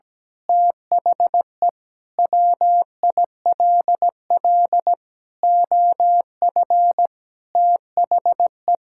Morse code, heard as a keyed tone, about 17 wpm, 700 Hz. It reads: THE WILL OF THE
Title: Sentences from Top 1000 Words - Review Part 5 - 17wpm